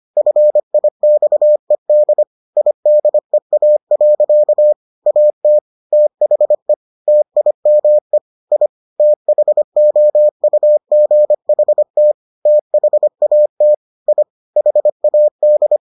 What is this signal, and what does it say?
Morse code, heard as a keyed tone, about 25 wpm, 600 Hz. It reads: FIXED IDEA. AT THE TIME I THOUGHT THAT I HAD